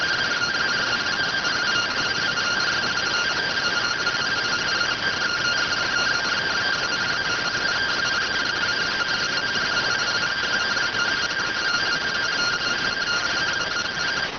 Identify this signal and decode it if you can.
Two-tone teleprinter signal, about 170 Hz shift.